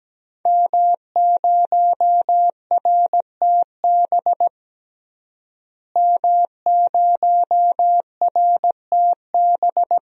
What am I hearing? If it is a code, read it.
Morse code, heard as a keyed tone, about 17 wpm, 700 Hz. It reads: M0RTB M0RTB
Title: DXCC-100 Call Signs - 5 Characters - Repeated 2x - 17wpm